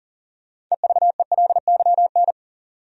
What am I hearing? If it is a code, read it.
Morse code, heard as a keyed tone, about 39 wpm, 700 Hz. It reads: EVELYN
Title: USA Common Names - 40wpm